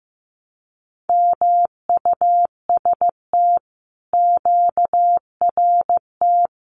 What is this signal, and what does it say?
Morse code, heard as a keyed tone, about 15 wpm, 700 Hz. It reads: MUST QRT
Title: Common QSO Phrases Encore 15wpm